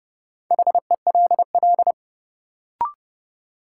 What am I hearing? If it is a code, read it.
Morse code, heard as a keyed tone, about 32 wpm, 700 Hz. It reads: HELL
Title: Sets of 5 Words Encore - New Word Drill - 30wpm